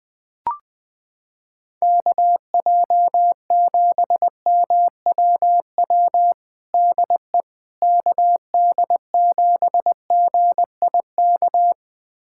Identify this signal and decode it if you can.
Morse code, heard as a keyed tone, about 20 wpm, 700 Hz. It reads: KJ7MWW DE KD7GIK
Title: USA Callsign DE Callsign - 6 Characters - 20wpm